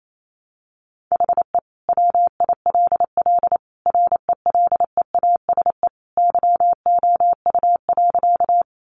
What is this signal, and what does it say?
Morse code, heard as a keyed tone, about 28 wpm, 700 Hz. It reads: HE WILL RELEASE YOU.
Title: Sentences from Top 1100 Words 28wpm